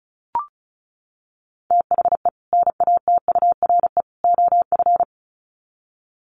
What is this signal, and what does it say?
Morse code, heard as a keyed tone, about 34 wpm, 700 Hz. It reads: THE NATURE OF
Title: Sets of 3 Words 35wpm